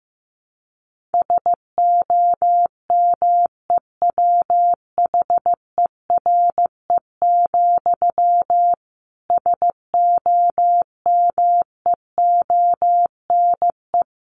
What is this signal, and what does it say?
Morse code, heard as a keyed tone, about 15 wpm, 700 Hz. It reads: SOMEWHERE, SOMEONE
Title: A Sound Of Thunder 15wpm